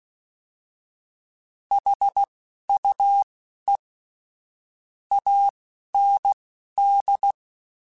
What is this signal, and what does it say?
Morse code, heard as a keyed tone, about 16 wpm, spaced as for 11 wpm, 800 Hz. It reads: HUE AND
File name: Misc / Misc-mp3s / Word Pairs @10